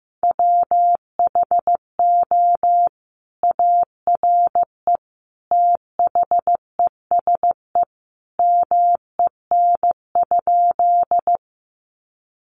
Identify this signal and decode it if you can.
Morse code, heard as a keyed tone, about 15 wpm, 700 Hz. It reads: WHO ARE THESE MEN?
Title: Sentences from Top 200 Words 15wpm